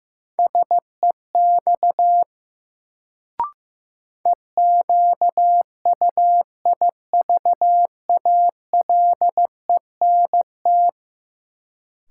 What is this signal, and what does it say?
Morse code, heard as a keyed tone, about 16 wpm, 700 Hz. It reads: SEX EQUIVALENT
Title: Sentences from Top 1600 Words - New Word Drill Alt - 15wpm